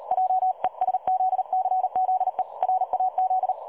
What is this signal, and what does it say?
Morse code, heard as a keyed tone, about 39 wpm, 750 Hz. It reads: JESZCZE RAZ
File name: dlazaawansowanych / nagraniazpasma / websdr recording 3565.0kHz